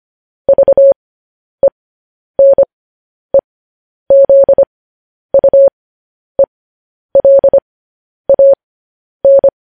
Morse code, heard as a keyed tone, about 25 wpm, spaced as for 10 wpm, 550 Hz. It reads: VENEZUELAN
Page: news headlines in morse code at 10 wpm